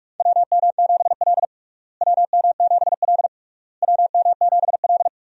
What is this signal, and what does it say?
Morse code, heard as a keyed tone, about 42 wpm, 700 Hz. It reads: WM7L WM7L WM7L
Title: USA Call Signs - 4 Characters - Repeated 3x - 45wpm